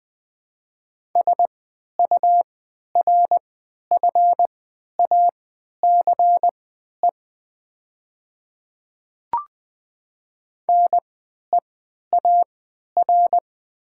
Morse code, heard as a keyed tone, about 20 wpm, spaced as for 11 wpm, 700 Hz. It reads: SURFACE NEAR
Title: 3x Character Spacing - Top 500 Words - 20wpm